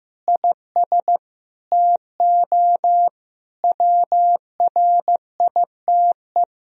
Morse code, heard as a keyed tone, about 15 wpm, 700 Hz. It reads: IS TO WRITE
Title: Sentences from Top 1000 Words - Review Part 3 - 15wpm